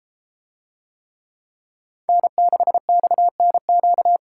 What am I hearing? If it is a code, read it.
Morse code, heard as a keyed tone, about 32 wpm, 700 Hz. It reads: N6XNQ